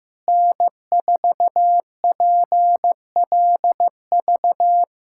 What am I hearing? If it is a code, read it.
Morse code, heard as a keyed tone, about 15 wpm, 700 Hz. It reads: N4PLV
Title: USA Callsign DE Callsign - 5 Characters - 15wpm